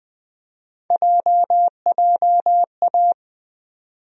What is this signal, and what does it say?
Morse code, heard as a keyed tone, about 20 wpm, 700 Hz. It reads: JJA
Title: USA Call Sign Postfixes - 3 Letters 20wpm